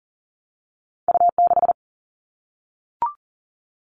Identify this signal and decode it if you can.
Morse code, heard as a keyed tone, about 41 wpm, 700 Hz. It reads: U6
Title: All Cut Numbers - 2-Digit 40wpm